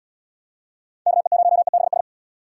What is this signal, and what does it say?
Morse code, heard as a keyed tone, about 79 wpm, 700 Hz. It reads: KE0EBS